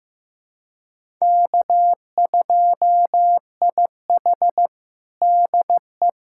Morse code, heard as a keyed tone, about 15 wpm, 700 Hz. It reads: K2IH DE
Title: USA Callsign DE Callsign - 4 Characters - 15wpm